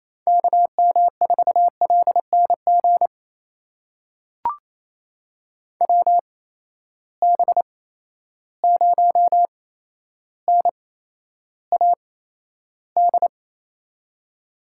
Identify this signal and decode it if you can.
Morse code, heard as a keyed tone, about 27 wpm, spaced as for 12 wpm, 700 Hz. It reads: KM4LNG WB0NAD
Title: USA Call Signs - 6 Characters - 8x Character Spacing - 28wpm